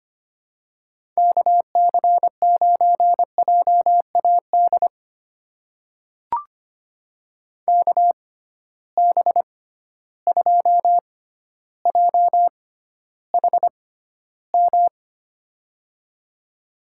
Morse code, heard as a keyed tone, about 25 wpm, spaced as for 13 wpm, 700 Hz. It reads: KC9JAD KB2JHM
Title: USA Call Signs - 6 Characters - 6x Character Spacing - 25wpm